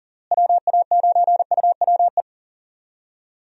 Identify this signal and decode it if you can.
Morse code, heard as a keyed tone, about 37 wpm, 700 Hz. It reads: WA9UWE
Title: USA Call Signs - 6 Characters - Repeated 3x - 40wpm